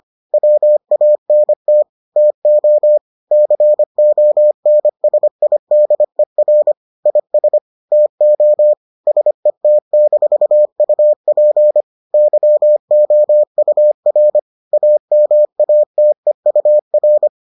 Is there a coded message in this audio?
Yes: Morse code from 0.3 to 17.3 s, heard as a keyed tone, about 25 wpm, 600 Hz. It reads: WANT TO CONSIDER IS TO SET-UP YOUR AMATEUR